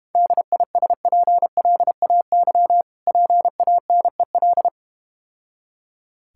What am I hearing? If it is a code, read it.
Morse code, heard as a keyed tone, about 32 wpm, 700 Hz. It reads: DISPLAY PANEL